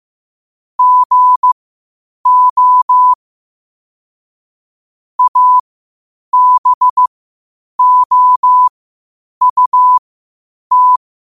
Morse code, heard as a keyed tone, about 15 wpm, spaced as for 8 wpm, 1000 Hz. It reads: GO ABOUT
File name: qotdfiles / 8wpm